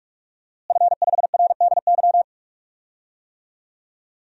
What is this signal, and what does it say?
Morse code, heard as a keyed tone, about 45 wpm, 700 Hz. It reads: F5RDY
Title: DXCC-100 Call Signs - 5 Characters - Repeated 3x - 45wpm